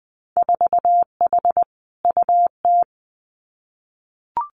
Morse code, heard as a keyed tone, about 20 wpm, 700 Hz. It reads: 4H UT
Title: Winter Field Day Exchanges Encore 20wpm